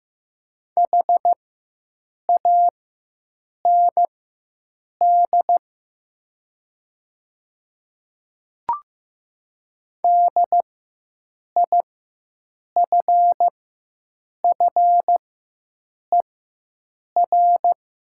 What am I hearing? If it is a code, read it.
Morse code, heard as a keyed tone, about 15 wpm, spaced as for 7 wpm, 700 Hz. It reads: HAND DIFFER
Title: 4x Character Spacing - Top 200 Words - 15wpm